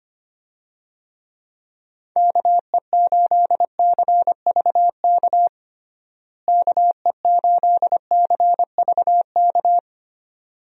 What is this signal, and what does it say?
Morse code, heard as a keyed tone, about 25 wpm, 700 Hz. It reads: KE8CVK KE8CVK
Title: USA Call Signs - 6 Characters - Repeated 2x - 25wpm